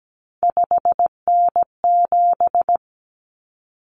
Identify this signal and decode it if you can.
Morse code, heard as a keyed tone, about 17 wpm, 700 Hz. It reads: 5N7
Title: Common Cut Numbers - 3-Digit 17wpm